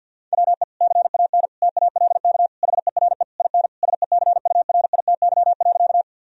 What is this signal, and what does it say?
Morse code, heard as a keyed tone, about 50 wpm, 700 Hz. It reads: WE CAN TALK HERE IN SECURITY.
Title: Sentences from Top 1000 Words - Review Part 4 - 50wpm